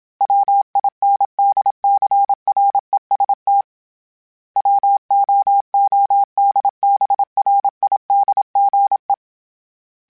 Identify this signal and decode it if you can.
Morse code, heard as a keyed tone, about 27 wpm, 800 Hz. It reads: WINDCREST WOODBRIDGE
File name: fromSMs / FromSMs mp3s / Cities@25